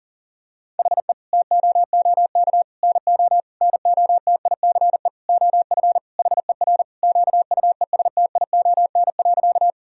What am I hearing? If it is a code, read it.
Morse code, heard as a keyed tone, about 40 wpm, 700 Hz. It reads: HE TOOK NO NOTICE OF HER QUESTION.
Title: Sentences from Top 1000 Words - Review Part 2 - 40wpm